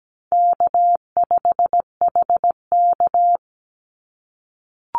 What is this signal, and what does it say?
Morse code, heard as a keyed tone, about 17 wpm, 700 Hz. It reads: K5HK
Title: Easy Call Signs - 17wpm